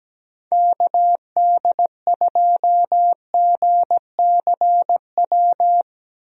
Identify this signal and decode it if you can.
Morse code, heard as a keyed tone, about 17 wpm, 700 Hz. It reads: KD2GCW
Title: USA Call Signs - 6 Characters - 17wpm